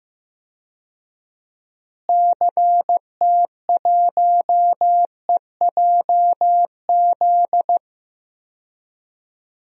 Morse code, heard as a keyed tone, about 15 wpm, 700 Hz. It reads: CT1EJZ
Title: DXCC-100 Call Signs - 6 Characters - Repeated 2x - 15wpm